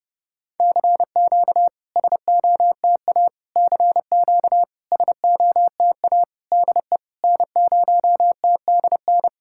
Morse code, heard as a keyed tone, about 32 wpm, 700 Hz. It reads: CQ SOTA CQ SOTA DE N0TDN
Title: SOTA QSO Practice 30wpm